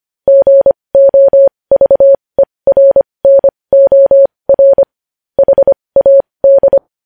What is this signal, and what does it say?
Morse code, heard as a keyed tone, about 25 wpm, spaced as for 20 wpm, 550 Hz. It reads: GOVERNOR HAD
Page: News Headlines in Morse Code at 20 WPM